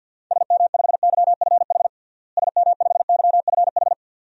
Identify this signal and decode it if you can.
Morse code, heard as a keyed tone, about 51 wpm, 700 Hz. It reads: SM5YFH SM5YFH
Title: DXCC-100 Call Signs - 6 Characters - Repeated 2x - 50wpm